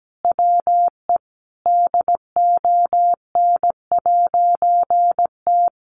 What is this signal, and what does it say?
Morse code, heard as a keyed tone, about 17 wpm, 700 Hz. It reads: WE DON'T
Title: A Sound Of Thunder 17wpm